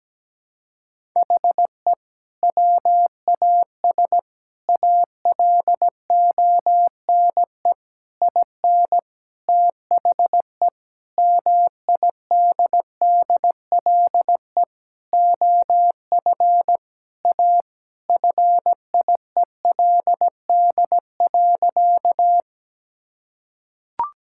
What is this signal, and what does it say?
Morse code, heard as a keyed tone, about 17 wpm, 700 Hz. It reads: HE WAS ALONE IN THE MIDDLE OF A FIELD.
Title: Sentences from Top 1000 Words - Review Part 6 - 17wpm